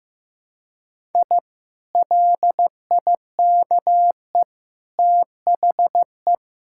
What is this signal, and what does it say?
Morse code, heard as a keyed tone, about 15 wpm, 700 Hz. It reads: I LIKE THE
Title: Sentences from Top 1000 Words - Review Part 6 - 15wpm